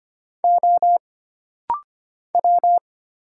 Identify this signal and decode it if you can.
Morse code, heard as a keyed tone, about 25 wpm, 700 Hz. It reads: O W
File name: Single Character (Letter:Number:Punctuation:Prosign) - Rapid-Fire - 25wpm